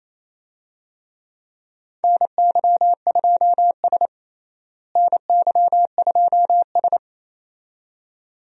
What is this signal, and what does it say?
Morse code, heard as a keyed tone, about 27 wpm, 700 Hz. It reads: NY2S NY2S
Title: USA Call Signs - 4 Characters - Repeated 2x - 28wpm